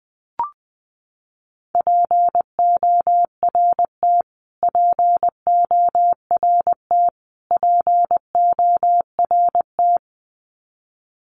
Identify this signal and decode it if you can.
Morse code, heard as a keyed tone, about 20 wpm, 700 Hz. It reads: PORT PORT PORT
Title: Top 500 Words Repeated 20wpm